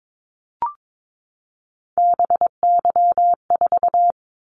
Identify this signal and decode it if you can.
Morse code, heard as a keyed tone, about 22 wpm, 700 Hz. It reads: BY4
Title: Callsign Prefixes for Top 100 DXCC - 22wpm